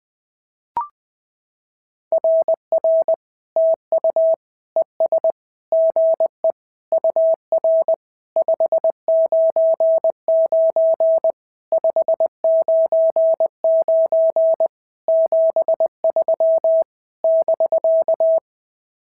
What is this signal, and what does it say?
Morse code, heard as a keyed tone, about 19 wpm, 650 Hz. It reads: RR TU ES GE UR 599 599 73 <BK>
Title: SOTA QSO Practice 20wpm